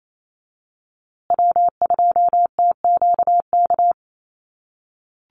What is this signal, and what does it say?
Morse code, heard as a keyed tone, about 28 wpm, 690 Hz. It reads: W2TQK
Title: USA Call Signs Repeated 3x 28wpm